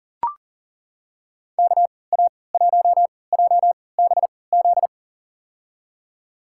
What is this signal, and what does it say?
Morse code, heard as a keyed tone, about 39 wpm, spaced as for 22 wpm, 700 Hz. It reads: KA1JBZ
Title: USA Call Signs - 6 Characters - 3x Character Spacing - 40wpm